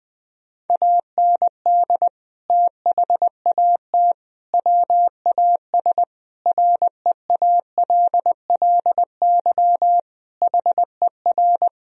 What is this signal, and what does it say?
Morse code, heard as a keyed tone, about 21 wpm, 700 Hz. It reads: AND THAT WAS REALLY HER
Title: Sentences from Top 1000 Words - Review Part 3 - 20wpm